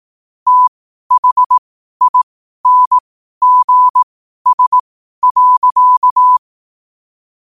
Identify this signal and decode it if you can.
Morse code, heard as a keyed tone, about 18 wpm, spaced as for 12 wpm, 1000 Hz. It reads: THINGS.
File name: qotdfiles / 12wpm